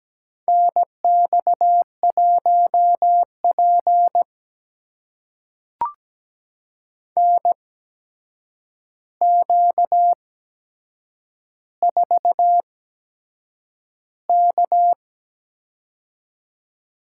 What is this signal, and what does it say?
Morse code, heard as a keyed tone, about 17 wpm, spaced as for 7 wpm, 700 Hz. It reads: NX1P NQ4K
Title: USA Call Signs - 4 Characters - 8x Character Spacing - 17wpm